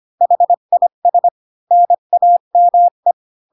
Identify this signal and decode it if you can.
Morse code, heard as a keyed tone, about 25 wpm, 700 Hz. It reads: HIS NAME
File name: Pinocchio / Pinocchio ch2 20 25 ik1hge